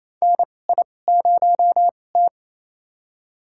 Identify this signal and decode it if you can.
Morse code, heard as a keyed tone, about 28 wpm, spaced as for 20 wpm, 700 Hz. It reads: NI0T
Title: USA Call Signs - 2x Character Spacing - 28wpm